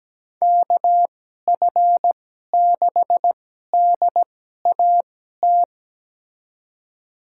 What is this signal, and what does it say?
Morse code, heard as a keyed tone, about 17 wpm, spaced as for 12 wpm, 700 Hz. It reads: KF6DAT
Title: USA Call Signs - 6 Characters - 2x Character Spacing - 17wpm